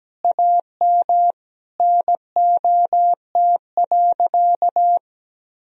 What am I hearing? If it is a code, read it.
Morse code, heard as a keyed tone, about 17 wpm, 700 Hz. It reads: AM NOT.
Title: Sentences from Top 1000 Words - Review Part 2 - 17wpm